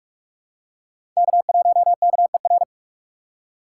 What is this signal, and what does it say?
Morse code, heard as a keyed tone, about 42 wpm, 700 Hz. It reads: K1KER
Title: USA Call Signs - 5 Characters - 2x Character Spacing - 45wpm